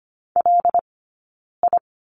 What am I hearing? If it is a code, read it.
Morse code, heard as a keyed tone, about 25 wpm, spaced as for 9 wpm, 690 Hz. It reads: LI
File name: Lesson 15d - Number 5 - Two Characters 25wpm